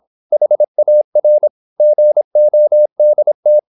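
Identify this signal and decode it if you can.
Morse code, heard as a keyed tone, about 26 wpm, 600 Hz. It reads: HAR GODT
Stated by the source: High-speed (QRQ) Morse at 26 wpm.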